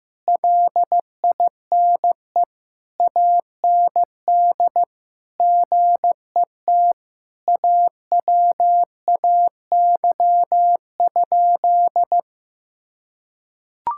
Morse code, heard as a keyed tone, about 15 wpm, 700 Hz. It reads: LINE AND GET AWAY?
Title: Sentences from Top 1000 Words - Review Part 6 - 15wpm